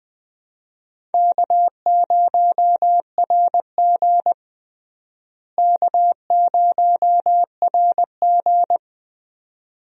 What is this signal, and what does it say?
Morse code, heard as a keyed tone, about 20 wpm, 700 Hz. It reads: K0RG K0RG
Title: USA Call Signs - 4 Characters - Repeated 2x - 20wpm